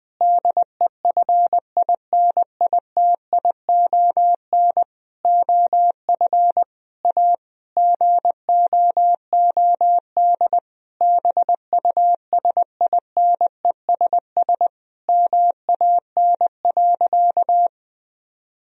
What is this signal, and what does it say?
Morse code, heard as a keyed tone, about 20 wpm, 700 Hz. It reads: DEFINITION OF A GOOD BUSINESS MAN.
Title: Sentences from Top 1000 Words - Review Part 4 - 20wpm